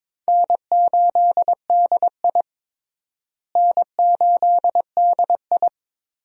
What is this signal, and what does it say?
Morse code, heard as a keyed tone, about 22 wpm, 700 Hz. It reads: N8DI N8DI
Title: USA Call Signs - 4 Characters - Repeated 3x - 22wpm